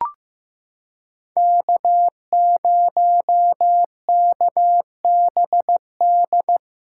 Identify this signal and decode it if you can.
Morse code, heard as a keyed tone, about 16 wpm, 700 Hz. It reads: K0KBD
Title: USA Callsign DE Callsign - 5 Characters - 15wpm